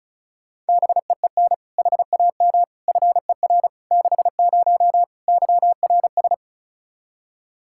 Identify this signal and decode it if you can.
Morse code, heard as a keyed tone, about 36 wpm, 700 Hz. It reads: BEEN HAM FER 60 YRS
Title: K1USN SST QSO Practice 35wpm